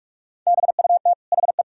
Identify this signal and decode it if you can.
Morse code, heard as a keyed tone, about 41 wpm, 700 Hz. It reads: BUT HE
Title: Sentences from Top 1600 Words 45wpm